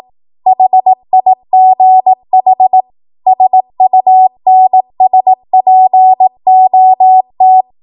Morse code, heard as a keyed tone, about 18 wpm, 750 Hz. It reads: HIGH SUNSPOT
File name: Morse / 18WPM